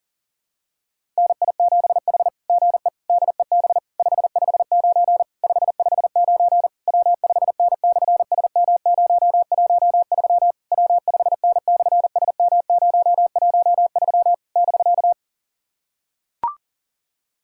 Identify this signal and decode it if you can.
Morse code, heard as a keyed tone, about 40 wpm, 700 Hz. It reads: NI7H GE DEB 559 559 W5N/SM013 W5N/SM013 <BK>
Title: SOTA QSO Practice 40wpm